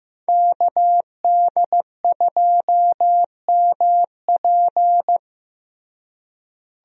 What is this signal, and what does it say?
Morse code, heard as a keyed tone, about 15 wpm, 700 Hz. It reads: KD2MP
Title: USA Call Signs - 5 Characters - Repeated 2x - 15wpm